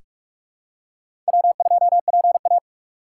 Morse code, heard as a keyed tone, about 44 wpm, 700 Hz. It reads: W2PA
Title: USA Call Signs - 4 Characters - 6x Character Spacing - 45wpm